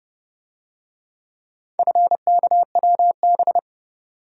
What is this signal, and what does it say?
Morse code, heard as a keyed tone, about 29 wpm, 700 Hz. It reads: FKWB